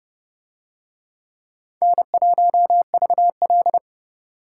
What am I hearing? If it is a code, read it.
Morse code, heard as a keyed tone, about 28 wpm, 700 Hz. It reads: N1VL